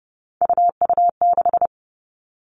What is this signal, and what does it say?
Morse code, heard as a keyed tone, about 30 wpm, 700 Hz. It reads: UU6
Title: Three Challenging Letter-Number Combinations 30wpm